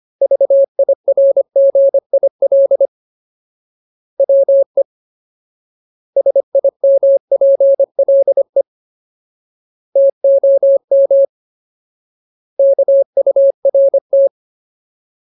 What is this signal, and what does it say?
Morse code, heard as a keyed tone, about 25 wpm, 550 Hz. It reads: VIRGIL WE SIMPLE TOM KURT
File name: Ultimate Word List / 25WPM Ultimate